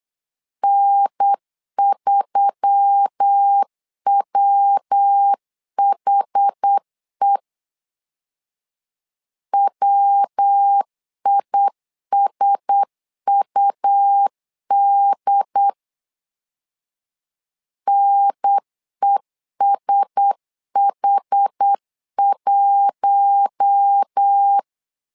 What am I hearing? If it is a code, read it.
Morse code, heard as a keyed tone, about 8 wpm, 800 Hz. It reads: N3WHE WISUD NESH1
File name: LEZIONI / 4 Q1S3 8wpm 3037